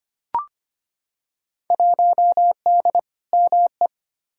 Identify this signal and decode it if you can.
Morse code, heard as a keyed tone, about 25 wpm, 700 Hz. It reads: 1D ME
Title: Field Day Exchanges Encore - 25wpm